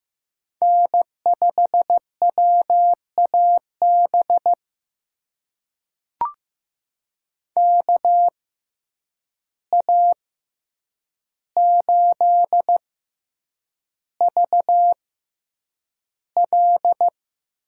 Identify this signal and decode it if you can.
Morse code, heard as a keyed tone, about 15 wpm, spaced as for 8 wpm, 700 Hz. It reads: N5WAB KA8VL
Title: USA Call Signs - 6x Character Spacing - 15wpm